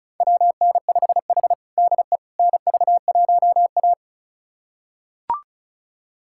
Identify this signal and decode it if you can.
Morse code, heard as a keyed tone, about 36 wpm, 700 Hz. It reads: WN5H DE NV1A
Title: USA Callsign DE Callsign - 4 Characters - 35wpm